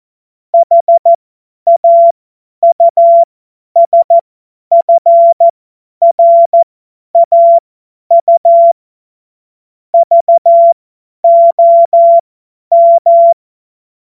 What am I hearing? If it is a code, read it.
Morse code, heard as a keyed tone, about 14 wpm, spaced as for 10 wpm, 680 Hz. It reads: HAUSFRAU VOM